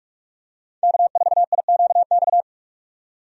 Morse code, heard as a keyed tone, about 41 wpm, 700 Hz. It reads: K4IQX
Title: USA Call Signs - 5 Characters - 45wpm